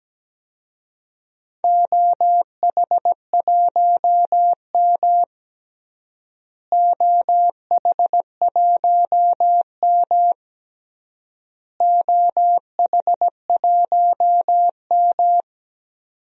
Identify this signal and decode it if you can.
Morse code, heard as a keyed tone, about 17 wpm, 700 Hz. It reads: OH1M OH1M OH1M
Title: DXCC-100 Call Signs - 4 Characters - Repeated 3x - 17wpm